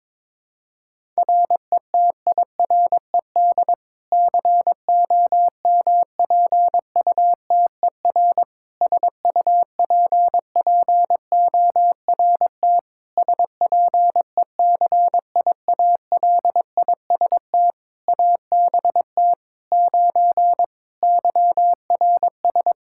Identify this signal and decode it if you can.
Morse code, heard as a keyed tone, about 22 wpm, 700 Hz. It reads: RETIRED COMPUTER SUPPORT SPECIALIST ABT 9 YRS
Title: Practice QSO - Exchange 3 - More Info - 22wpm